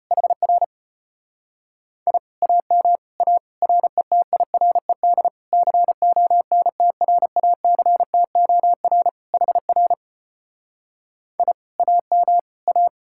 Morse code, heard as a keyed tone, about 33 wpm, 700 Hz. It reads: HR I AM A RETIRED CONTRACTOR HR I AM A